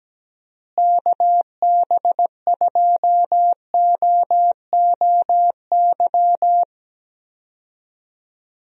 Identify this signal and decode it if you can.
Morse code, heard as a keyed tone, about 17 wpm, 700 Hz. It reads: KB2OOY
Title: USA Call Signs - 6 Characters - Repeated 2x - 17wpm